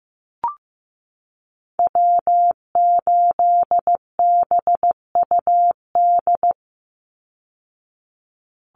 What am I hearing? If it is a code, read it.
Morse code, heard as a keyed tone, about 15 wpm, 700 Hz. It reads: W8BUD
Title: Call Signs Repeated 3x - 15wpm